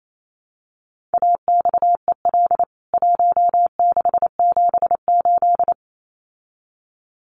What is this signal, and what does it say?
Morse code, heard as a keyed tone, about 28 wpm, 700 Hz. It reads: AXEL 1678
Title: CWT Exchange 28wpm